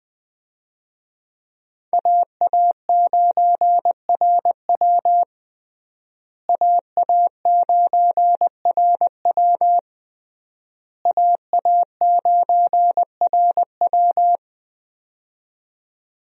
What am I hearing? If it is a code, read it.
Morse code, heard as a keyed tone, about 20 wpm, 700 Hz. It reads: AA9RW AA9RW AA9RW
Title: USA Call Signs - 5 Characters - Repeated 3x - 20wpm